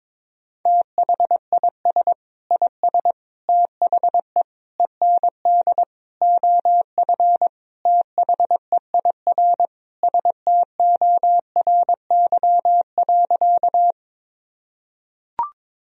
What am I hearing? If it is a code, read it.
Morse code, heard as a keyed tone, about 22 wpm, 700 Hz. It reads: THIS IS THE END OF THEIR STORY.
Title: Sentences from Top 1000 Words - Review Part 1 - 22wpm